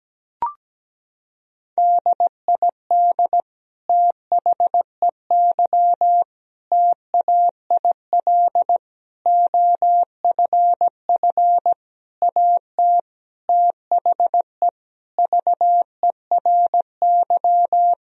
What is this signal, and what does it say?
Morse code, heard as a keyed tone, about 17 wpm, 700 Hz. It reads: DID THEY TAIL OFF AT THE VERY
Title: Sentences from Top 1000 Words - Review Part 3 - 17wpm